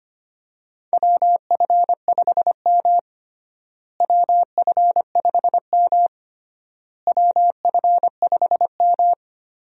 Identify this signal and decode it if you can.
Morse code, heard as a keyed tone, about 25 wpm, 700 Hz. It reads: WF5M WF5M WF5M
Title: USA Call Signs - 4 Characters - Repeated 3x - 25wpm